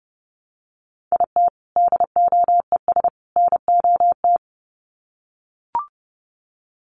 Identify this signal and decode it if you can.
Morse code, heard as a keyed tone, about 30 wpm, 700 Hz. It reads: IT DOES NOT
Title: Sets of 3 Words 30wpm